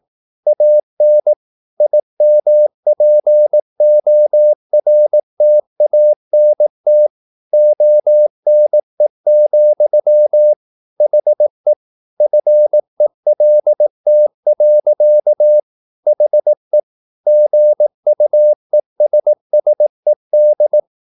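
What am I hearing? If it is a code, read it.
Morse code, heard as a keyed tone, about 18 wpm, 600 Hz. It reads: AN IMPORTANT ONE, HE FELT. HE GUESSED
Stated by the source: Morse practice at 18 wpm, English text, from CW med Gnister.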